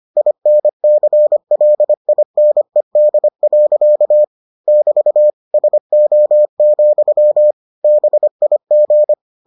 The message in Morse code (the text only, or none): INCLINED. <BT> SO, BIG